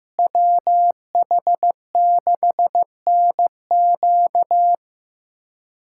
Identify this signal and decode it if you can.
Morse code, heard as a keyed tone, about 15 wpm, 700 Hz. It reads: WH6NQ
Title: USA Callsign DE Callsign - 5 Characters - 15wpm